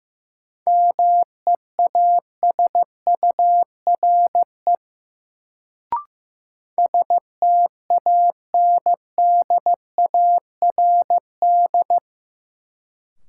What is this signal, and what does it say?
Morse code, heard as a keyed tone, about 15 wpm, 700 Hz. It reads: MEASURE STANDARD
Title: Sentences from Top 1200 Words - New Word Drill Alt - 15wpm